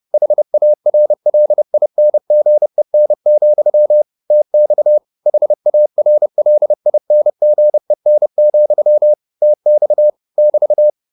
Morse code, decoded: HARLINGEN, TX HARLINGEN, TX <BT>

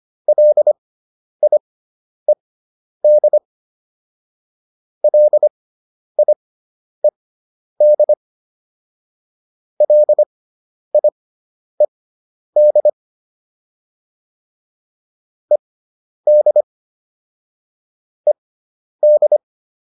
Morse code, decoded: LIED LIED LIED ED ED

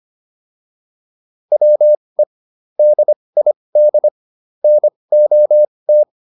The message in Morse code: WE DID NOT

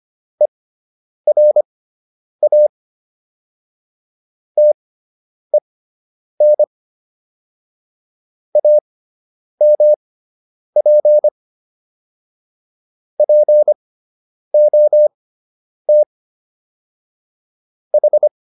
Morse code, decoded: ERA TEN AMP POT H

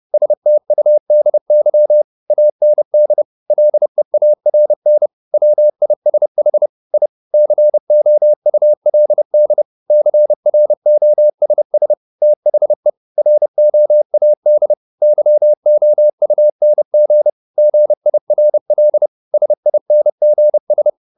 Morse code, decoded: STUDY AND LEARN WISH I COULD CROSS THE ROAD YOUNG GIRL SINGS

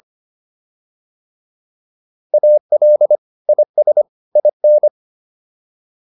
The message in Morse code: AL IS IN

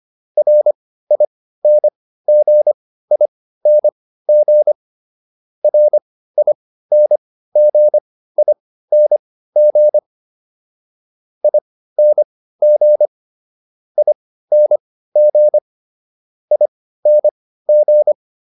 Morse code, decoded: RINGING RINGING ING ING ING

